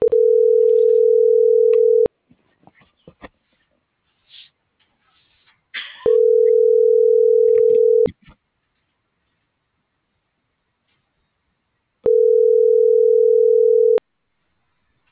Ringback tone (North America), 440 + 480 Hz.
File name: CENTRAL GOMSA / out aylin.bazan